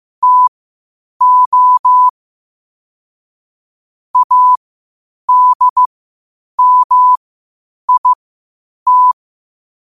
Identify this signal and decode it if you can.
Morse code, heard as a keyed tone, about 15 wpm, spaced as for 8 wpm, 1000 Hz. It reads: TO ADMIT
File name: qotdfiles / 8wpm